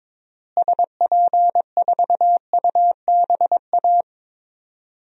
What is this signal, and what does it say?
Morse code, heard as a keyed tone, about 22 wpm, 700 Hz. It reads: SP4UBA
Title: DXCC-100 Call Signs - 6 Characters - Repeated 2x - 22wpm